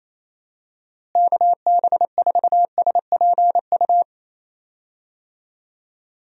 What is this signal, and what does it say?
Morse code, heard as a keyed tone, about 28 wpm, 700 Hz. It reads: KB4SPU
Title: USA Call Signs - 6 Characters - Repeated 3x - 28wpm